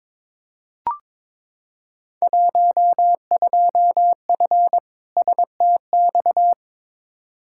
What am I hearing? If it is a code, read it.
Morse code, heard as a keyed tone, about 22 wpm, 700 Hz. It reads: 12F STX
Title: Field Day Exchanges Encore - 22wpm